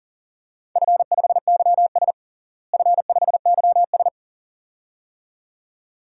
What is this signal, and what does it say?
Morse code, heard as a keyed tone, about 37 wpm, 700 Hz. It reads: F5YS F5YS
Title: DXCC-100 Call Signs - 4 Characters - Repeated 2x - 40wpm